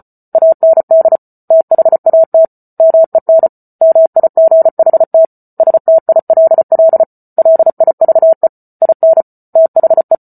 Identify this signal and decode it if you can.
Morse code, heard as a keyed tone, about 34 wpm, 650 Hz. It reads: AND THAT MEN MIGHT STILL LIVE IN THE